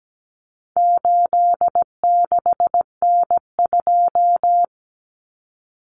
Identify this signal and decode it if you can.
Morse code, heard as a keyed tone, about 17 wpm, 700 Hz. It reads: 86N2
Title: Common Cut Numbers - 4-Digit 17wpm